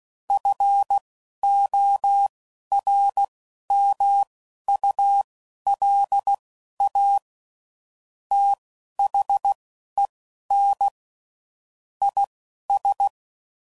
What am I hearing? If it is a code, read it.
Morse code, heard as a keyed tone, about 16 wpm, spaced as for 11 wpm, 800 Hz. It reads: FORMULA THEN IS
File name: CW Code Groups @10WPM / Technical Subjects / XC